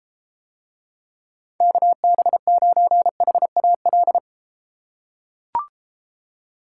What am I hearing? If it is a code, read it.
Morse code, heard as a keyed tone, about 33 wpm, 700 Hz. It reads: KB9HAL